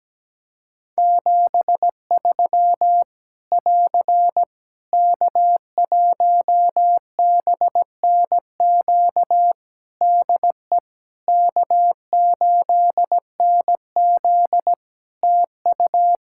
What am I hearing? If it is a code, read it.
Morse code, heard as a keyed tone, about 17 wpm, 700 Hz. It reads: 73 <AR> K1BNQ DE K8NZ TU
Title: Practice QSO - Exchange 4 - Ending - 17wpm